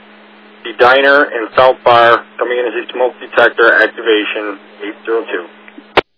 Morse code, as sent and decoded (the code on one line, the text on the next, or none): none